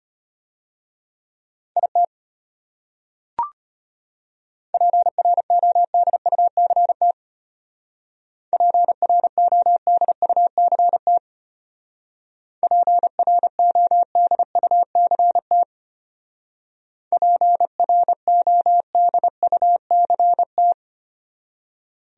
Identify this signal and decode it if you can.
Morse code, heard as a keyed tone, about 31 wpm, 700 Hz. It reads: IT PRODUCT PRODUCT PRODUCT PRODUCT
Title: Speed-Racing Top 500 Words 25wpm